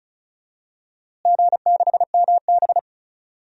Morse code, heard as a keyed tone, about 36 wpm, 700 Hz. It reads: G6MB
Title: DXCC-100 Call Signs - 4 Characters - Repeated 2x - 35wpm